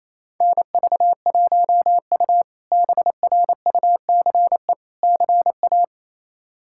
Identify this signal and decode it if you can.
Morse code, heard as a keyed tone, about 28 wpm, 700 Hz. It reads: NV1U BRUCE CA
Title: K1USN SST QSO Practice - 28wpm